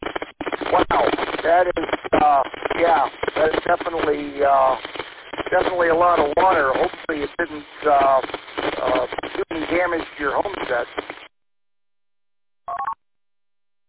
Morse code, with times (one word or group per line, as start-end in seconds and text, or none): none